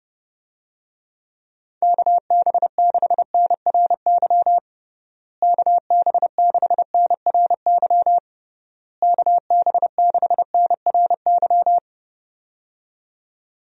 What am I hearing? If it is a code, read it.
Morse code, heard as a keyed tone, about 30 wpm, 700 Hz. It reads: KB6NRY KB6NRY KB6NRY
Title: USA Call Signs - 6 Characters - Repeated 3x - 30wpm